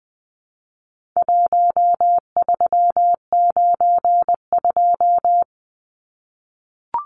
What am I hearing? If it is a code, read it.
Morse code, heard as a keyed tone, about 20 wpm, 700 Hz. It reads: 1392
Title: Numbers - Contest Style - 4-Digit 20wpm